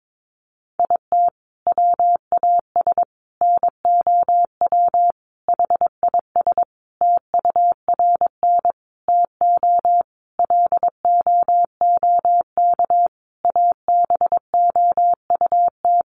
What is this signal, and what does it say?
Morse code, heard as a keyed tone, about 22 wpm, 700 Hz. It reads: IT WAS NOW HIS TURN TO LOOK ABOUT
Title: Sentences from Top 200 Words 22wpm